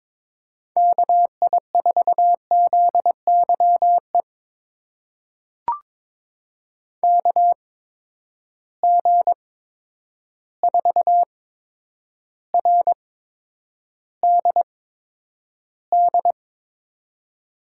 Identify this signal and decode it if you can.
Morse code, heard as a keyed tone, about 22 wpm, spaced as for 9 wpm, 700 Hz. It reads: KI4ZYE KG4RDD
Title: USA Call Signs - 6 Characters - 8x Character Spacing - 22wpm